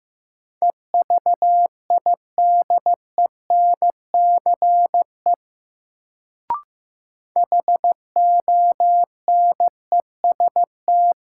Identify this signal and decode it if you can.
Morse code, heard as a keyed tone, about 15 wpm, 700 Hz. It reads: EVIDENCE HONEST
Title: Sets of 5 Words Encore - New Word Drill - 15wpm